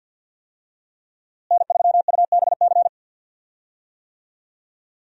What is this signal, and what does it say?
Morse code, heard as a keyed tone, about 52 wpm, 700 Hz. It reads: N3UBC
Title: USA Call Signs - 5 Characters - Repeated 3x - 50wpm